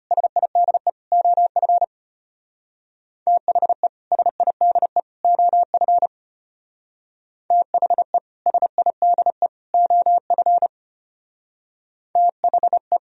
Morse code, heard as a keyed tone, about 30 wpm, 700 Hz. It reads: SIDE OF THE SIDE OF THE SIDE OF THE
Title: Speed-Racing - Sets of 3 Words 25wpm